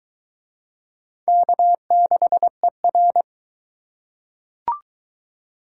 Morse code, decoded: K6ER

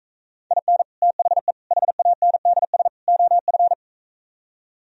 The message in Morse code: IN THE HANDS OF

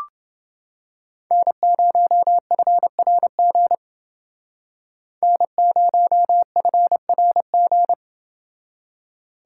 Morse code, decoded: N0FRG N0FRG

